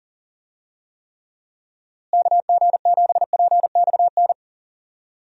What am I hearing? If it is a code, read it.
Morse code, heard as a keyed tone, about 40 wpm, 700 Hz. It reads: KG7PXN
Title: USA Call Signs - 6 Characters - Repeated 2x - 40wpm